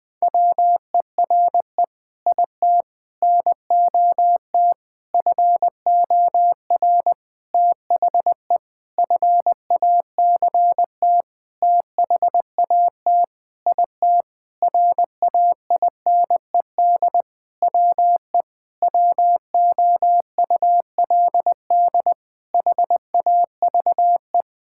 Morse code, heard as a keyed tone, about 21 wpm, 700 Hz. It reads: WERE IT NOT FOR THE FACT THAT IT RAINED WE WOULD HAVE
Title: Sentences with 7-Word Phrases - 20wpm